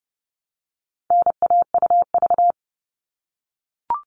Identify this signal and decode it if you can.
Morse code, heard as a keyed tone, about 30 wpm, 700 Hz. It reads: NAUV
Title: All Cut Numbers - 4-Digit 30wpm